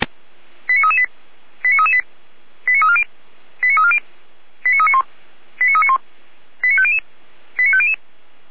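Five-tone selective call (ZVEI), heard as quick runs of five tones, tones of about 70 ms. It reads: 89338 89338 89340 89340 89391 89391 89500 89500
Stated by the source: ZVEI five-tone call received on SDR.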